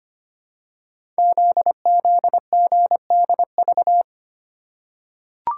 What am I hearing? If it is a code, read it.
Morse code, heard as a keyed tone, about 25 wpm, 700 Hz. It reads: ZZGDV
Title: Code Groups - 5 Letters - 25wpm